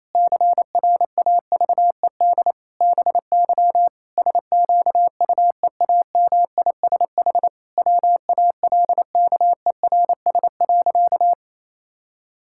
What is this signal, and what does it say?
Morse code, heard as a keyed tone, about 28 wpm, 700 Hz. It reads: CRAVED BY SQUEAMISH WALKERS.
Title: Pangram Challenge v2 - 28wpm